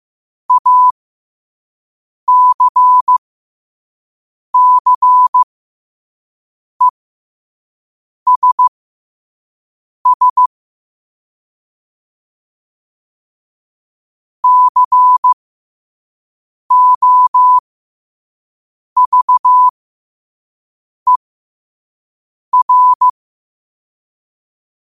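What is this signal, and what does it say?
Morse code, heard as a keyed tone, about 15 wpm, spaced as for 5 wpm, 1000 Hz. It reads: ACCESS COVER
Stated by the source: Morse code quote of the day sent at 5 wpm.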